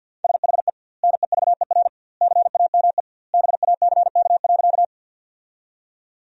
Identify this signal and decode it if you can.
Morse code, heard as a keyed tone, about 50 wpm, 700 Hz. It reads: SHE NEVER CAME BACK.
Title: 2x Word-Spacing - Sentences from Top 300 Words - 50wpm